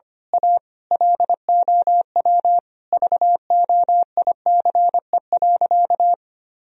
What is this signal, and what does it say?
Morse code, heard as a keyed tone, about 25 wpm, 700 Hz. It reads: A LOW VOICE.
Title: Sentences from Top 1000 Words - Review Part 1 - 25wpm